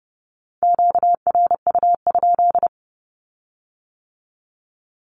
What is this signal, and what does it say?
Morse code, heard as a keyed tone, about 30 wpm, 690 Hz. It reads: QRU?
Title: Q-Codes with Spoken Meaning - 30wpm